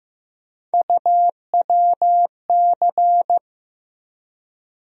Morse code, heard as a keyed tone, about 15 wpm, 700 Hz. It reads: UWC
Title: USA Call Sign Postfixes - 3 Letters 15wpm